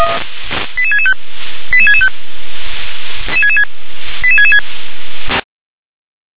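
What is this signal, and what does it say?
Five-tone selective call (ZVEI), heard as quick runs of five tones, tones of about 70 ms. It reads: 88665 88665 88666
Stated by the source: ZVEI five-tone selective call received via SDR.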